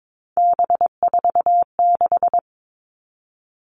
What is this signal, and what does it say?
Morse code, heard as a keyed tone, about 22 wpm, 700 Hz. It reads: B46
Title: All Cut Numbers - 3-Digit 22wpm